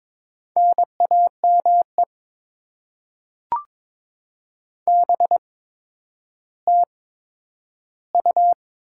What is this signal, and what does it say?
Morse code, heard as a keyed tone, about 22 wpm, spaced as for 11 wpm, 700 Hz. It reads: NAME BTU
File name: Top 20 QSO Elements - 8x Character Spacing - 22wpm